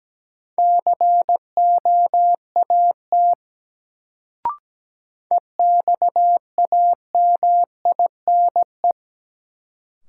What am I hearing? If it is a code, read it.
Morse code, heard as a keyed tone, about 17 wpm, 700 Hz. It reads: COAT EXAMINE
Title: Sentences from Top 1600 Words - New Word Drill Alt - 17wpm